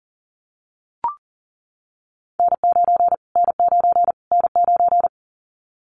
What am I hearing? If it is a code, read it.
Morse code, heard as a keyed tone, about 40 wpm, 700 Hz. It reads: N9 N9 N9
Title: Callsign Prefixes for Top 25 DXCC - Repeated 3x - 40wpm